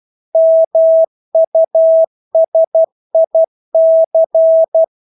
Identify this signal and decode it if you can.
Morse code, heard as a keyed tone, about 12 wpm, 650 Hz. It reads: MUSIC
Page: Engelske ord 12 wpm